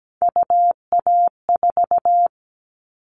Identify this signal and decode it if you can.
Morse code, heard as a keyed tone, about 17 wpm, 700 Hz. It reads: UA4
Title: Callsign Prefixes for Top 50 DXCC - 17wpm